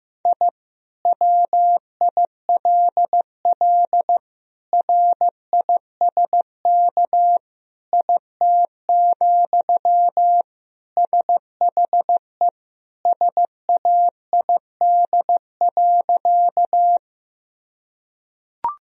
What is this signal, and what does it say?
Morse code, heard as a keyed tone, about 15 wpm, 700 Hz. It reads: I WILL RISK IT, SHE SAID.
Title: Sentences from Top 1000 Words - Review Part 4 - 15wpm